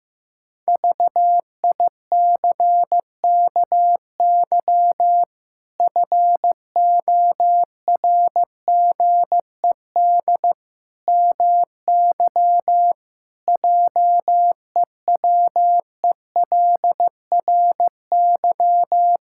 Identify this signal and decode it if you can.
Morse code, heard as a keyed tone, about 15 wpm, 700 Hz. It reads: VICKY FORGED MY JEWELRY
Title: Pangram Challenge v2 - 15wpm